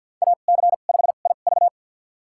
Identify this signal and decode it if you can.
Morse code, heard as a keyed tone, about 52 wpm, spaced as for 36 wpm, 700 Hz. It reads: AC5IV
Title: USA Call Signs - 5 Characters - 2x Character Spacing - 50wpm